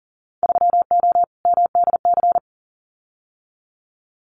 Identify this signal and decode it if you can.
Morse code, heard as a keyed tone, about 40 wpm, 700 Hz. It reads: 3O MDC
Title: Winter Field Day Exchanges Encore 40wpm